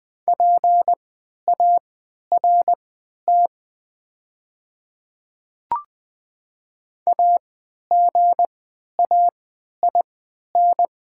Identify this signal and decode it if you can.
Morse code, heard as a keyed tone, about 20 wpm, spaced as for 11 wpm, 700 Hz. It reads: PART AGAIN
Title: 3x Character Spacing - Top 300 Words - 20wpm